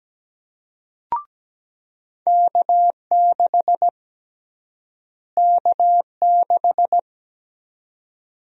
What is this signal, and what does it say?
Morse code, heard as a keyed tone, about 17 wpm, 700 Hz. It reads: K6 K6
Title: USA Call Sign Prefixes - Repeated 2x - 17wpm